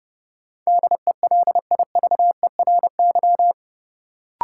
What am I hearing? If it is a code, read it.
Morse code, heard as a keyed tone, about 30 wpm, 700 Hz. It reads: DELIVERY
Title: Sentences from Top 1500 Words - New Word Drill Alt - 30wpm